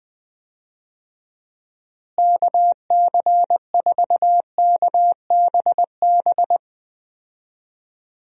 Morse code, heard as a keyed tone, about 20 wpm, 700 Hz. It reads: KC4KBB
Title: USA Call Signs - 6 Characters - Repeated 3x - 20wpm